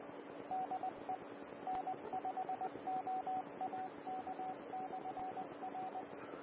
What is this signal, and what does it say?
Morse code, heard as a keyed tone, about 20 wpm, 750 Hz, with radio noise. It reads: DE N5OAK/R
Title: Austin, TX Amateur Repeaters